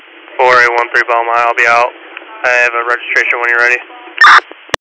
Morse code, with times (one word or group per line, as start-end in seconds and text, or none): none